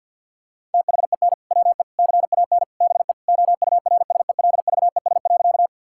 Morse code, heard as a keyed tone, about 50 wpm, 700 Hz. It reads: THEN WE CAN BE OURSELVES.
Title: Sentences from Top 1600 Words 50wpm